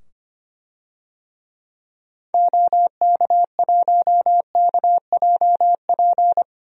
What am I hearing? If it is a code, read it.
Morse code, heard as a keyed tone, about 25 wpm, 700 Hz. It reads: OK1KJP
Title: DXCC-100 Call Signs - 6 Characters - 25wpm